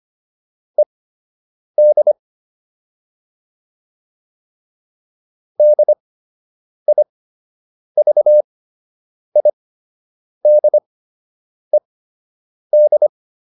Morse code, heard as a keyed tone, about 25 wpm, spaced as for 8 wpm, 600 Hz. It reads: ED DIVIDED